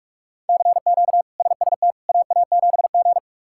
Morse code, heard as a keyed tone, about 46 wpm, 700 Hz. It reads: CQ SST AA7G
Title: K1USN SST QSO Practice - 45wpm